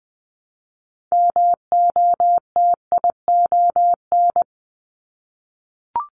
Morse code, heard as a keyed tone, about 20 wpm, 700 Hz. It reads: MOTION
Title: Multi-letter Phonemes Repeated 3x 20wpm